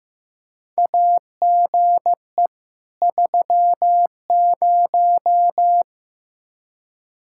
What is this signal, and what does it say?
Morse code, heard as a keyed tone, about 15 wpm, 700 Hz. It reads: AGE 30
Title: Practice QSO - Exchange 3 - More Info - 15wpm